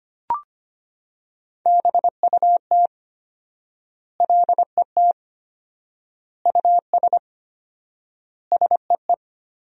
Morse code, heard as a keyed tone, about 25 wpm, 700 Hz. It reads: BUT LET US SEE
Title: 4x Word-Spacing - Sentences from Top 500 Words - 25wpm